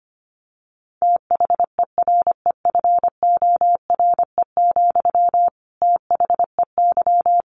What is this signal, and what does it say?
Morse code, heard as a keyed tone, about 25 wpm, 700 Hz. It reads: THEREFORE, THEY
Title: Sentences from Top 1200 Words 25wpm